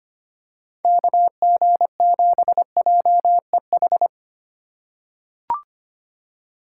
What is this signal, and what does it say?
Morse code, heard as a keyed tone, about 25 wpm, 700 Hz. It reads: KG7JEH
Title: USA Call Signs - 6 Characters - 4x Character Spacing - 25wpm